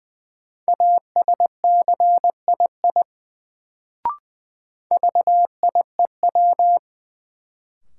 Morse code, heard as a keyed tone, about 20 wpm, 700 Hz. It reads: ASCII VIEW
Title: Sets of 5 Words - New Word Drill - 20wpm